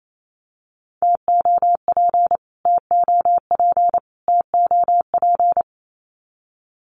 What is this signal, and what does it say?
Morse code, heard as a keyed tone, about 28 wpm, 700 Hz. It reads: TOP TOP TOP
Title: Top 500 Words Repeated 28wpm